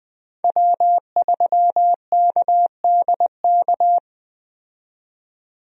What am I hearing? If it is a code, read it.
Morse code, heard as a keyed tone, about 20 wpm, 700 Hz. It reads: W3KDK
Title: USA Call Signs - 5 Characters - 20wpm